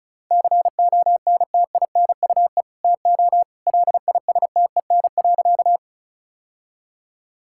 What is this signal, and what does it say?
Morse code, heard as a keyed tone, about 34 wpm, 700 Hz. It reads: CONTINUE TO LISTEN.
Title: Sentences from Top 1000 Words - Review Part 6 - 35wpm